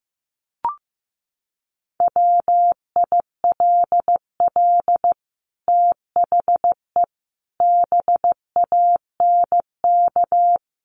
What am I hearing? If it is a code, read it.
Morse code, heard as a keyed tone, about 15 wpm, 700 Hz. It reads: WILL THE BANK
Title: Sentences from Top 1500 Words 15wpm